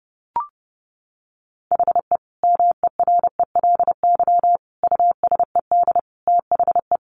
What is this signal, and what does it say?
Morse code, heard as a keyed tone, about 30 wpm, 700 Hz. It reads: HE MERELY USED THE
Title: Sentences from Top 1400 Words 30wpm